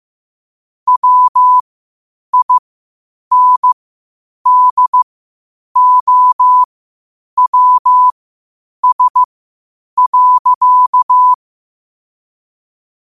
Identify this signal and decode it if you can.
Morse code, heard as a keyed tone, about 15 wpm, spaced as for 8 wpm, 1000 Hz. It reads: WINDOWS.
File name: 8wpm